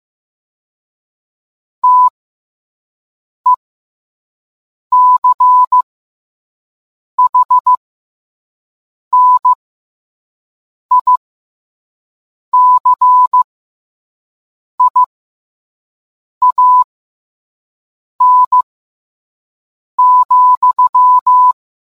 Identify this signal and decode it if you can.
Morse code, heard as a keyed tone, about 15 wpm, spaced as for 5 wpm, 1000 Hz. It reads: TECHNICIAN,